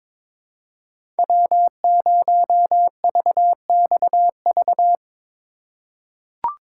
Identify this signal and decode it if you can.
Morse code, heard as a keyed tone, about 23 wpm, 700 Hz. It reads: W0VXV
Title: USA Call Signs - 5 Characters - 2x Character Spacing - 22wpm